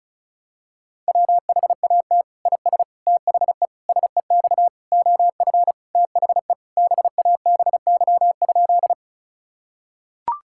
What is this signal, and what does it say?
Morse code, heard as a keyed tone, about 36 wpm, 700 Hz. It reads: WHAT IS THE SEX OF THE BABY?
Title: Sentences from Top 1600 Words 35wpm